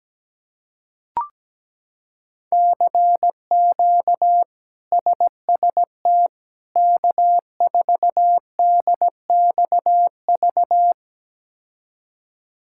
Morse code, heard as a keyed tone, about 17 wpm, 700 Hz. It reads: CQ SST K4DXV
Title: K1USN SST QSO Practice - 17wpm